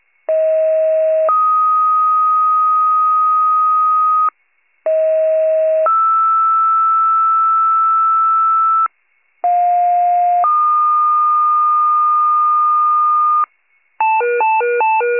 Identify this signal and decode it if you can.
Two-tone page, heard as a one-second tone, then a three-second tone, about 640 then 1250 Hz, about 640 then 1320 Hz, about 710 then 1220 Hz. It is heard from 0.3 to 13.4 s.